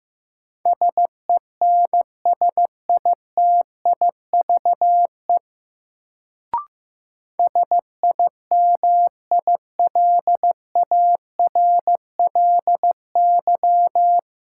Morse code, heard as a keyed tone, about 15 wpm, 700 Hz. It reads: SENSITIVE SIMILARLY
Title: Sentences from Top 1400 Words - New Word Drill Alt - 15wpm